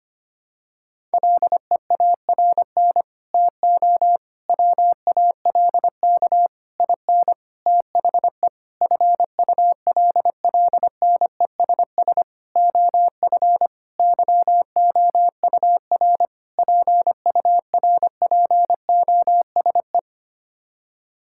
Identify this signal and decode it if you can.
Morse code, heard as a keyed tone, about 25 wpm, 700 Hz. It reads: LEARN TO WALK IN THE FULLNESS OF YOUR PURPOSE
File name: Sets of 9 Words - 25wpm